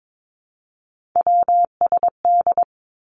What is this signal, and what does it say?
Morse code, heard as a keyed tone, about 22 wpm, 700 Hz. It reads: WSD
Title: Three Challenging Letter-Number Combinations 22wpm